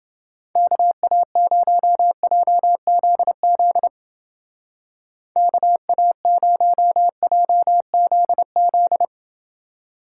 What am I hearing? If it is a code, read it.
Morse code, heard as a keyed tone, about 28 wpm, 700 Hz. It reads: KA0JZZ KA0JZZ